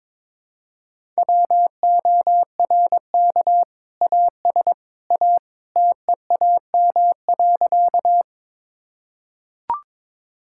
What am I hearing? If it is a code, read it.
Morse code, heard as a keyed tone, about 22 wpm, 700 Hz. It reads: WORK AS A TEAM.
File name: practice / 22wpm / Sentences with 2-Word Phrases - 22wpm